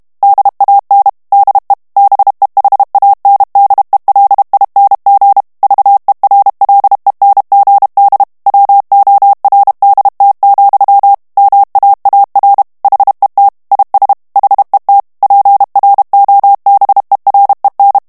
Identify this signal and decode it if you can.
Morse code, heard as a keyed tone, about 32 wpm, 800 Hz. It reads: DAN DE BEHANDELING VERLENGD WORDT, MAAR HET IS HET PROBEREN
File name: HOLLANDAIS texte clair 2 vitesse 30 mots minute